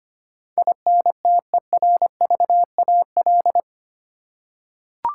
Morse code, heard as a keyed tone, about 25 wpm, 700 Hz. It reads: INTERVAL
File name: Eight Letter Words - 25wpm